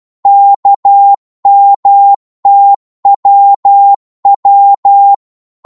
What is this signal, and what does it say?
Morse code, heard as a keyed tone, about 12 wpm, 800 Hz. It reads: KMTWW
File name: Morse / ltr-12wpm